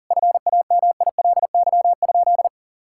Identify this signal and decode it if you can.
Morse code, heard as a keyed tone, about 40 wpm, 700 Hz. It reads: FAMILY?
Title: Sentences from Top 1000 Words - Review Part 6 - 40wpm